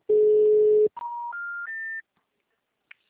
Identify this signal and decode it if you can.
Special information tone.